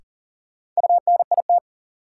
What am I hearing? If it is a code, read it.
Morse code, heard as a keyed tone, about 38 wpm, 700 Hz. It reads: UNIT
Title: Sentences from Top 1000 Words - Review Part 2 - New Word Drill - 40wpm